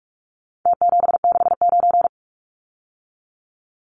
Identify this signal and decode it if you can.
Morse code, heard as a keyed tone, about 44 wpm, 700 Hz. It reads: T769
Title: Numbers - Contest Style - 4-Digit 45wpm